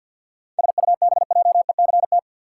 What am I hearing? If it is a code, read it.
Morse code, heard as a keyed tone, about 48 wpm, 700 Hz. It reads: SUBJECT
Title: Sentences from Top 800 Words - New Word Drill - 50wpm